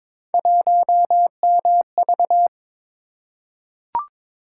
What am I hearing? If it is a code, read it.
Morse code, heard as a keyed tone, about 22 wpm, 700 Hz. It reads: 1MV
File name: practice / 22wpm / Three Letter-Number Combinations - 22wpm